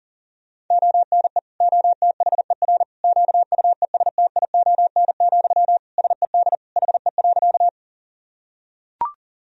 Morse code, heard as a keyed tone, about 40 wpm, 700 Hz. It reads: ONE OTHER QUESTION, SED HE.
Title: Sentences from Top 400 Words and Abbreviations 40wpm